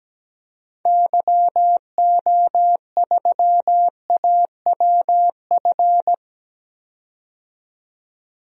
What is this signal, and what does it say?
Morse code, heard as a keyed tone, about 17 wpm, 700 Hz. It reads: YO3AWF
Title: DXCC-100 Call Signs - 6 Characters - 17wpm